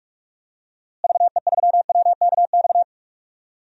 Morse code, heard as a keyed tone, about 46 wpm, 700 Hz. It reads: VE3WKX
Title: DXCC-100 Call Signs - 6 Characters - Repeated 3x - 45wpm